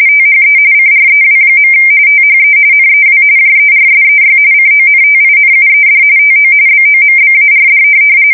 RTTY, about 45.45 baud. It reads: THE CREW. THE NEW JERSEY CONTINUED TO BOMBARD
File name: fsk170 / USS-New-Jersey